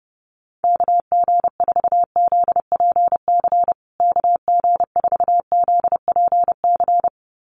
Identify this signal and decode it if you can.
Morse code, heard as a keyed tone, about 30 wpm, 700 Hz. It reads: KG4ZPC KG4ZPC
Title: USA Call Signs Repeated 2x 30wpm